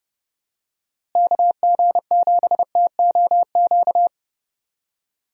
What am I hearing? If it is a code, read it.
Morse code, heard as a keyed tone, about 32 wpm, 700 Hz. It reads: KG7TOQ
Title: USA Call Signs - 6 Characters - Repeated 3x - 30wpm